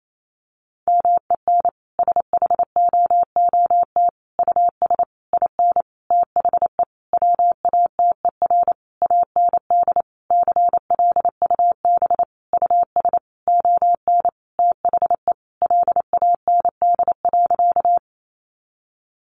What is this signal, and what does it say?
Morse code, heard as a keyed tone, about 28 wpm, 700 Hz. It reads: MEN SHOOT US IN THE WATER AND CLUB US ON THE LAND.
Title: Sentences from Top 1200 Words 28wpm